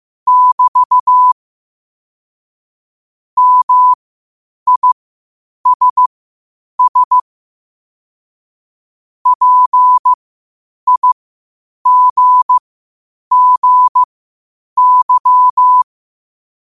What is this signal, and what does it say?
Morse code, heard as a keyed tone, about 15 wpm, spaced as for 8 wpm, 1000 Hz. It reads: <BT> MISS PIGGY
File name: qotdfiles / 8wpm